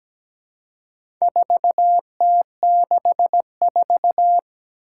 Morse code, heard as a keyed tone, about 17 wpm, 700 Hz. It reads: 4T64
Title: Four Challenging Letter-Number Combinations 17wpm